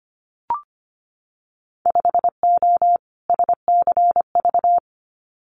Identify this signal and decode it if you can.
Morse code, heard as a keyed tone, about 25 wpm, 700 Hz. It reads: 5O SCV
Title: Winter Field Day Exchanges Encore 25wpm